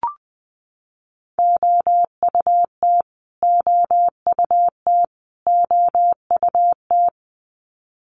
Morse code, heard as a keyed tone, about 20 wpm, 700 Hz. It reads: OUT OUT OUT
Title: Top 500 Words Repeated 20wpm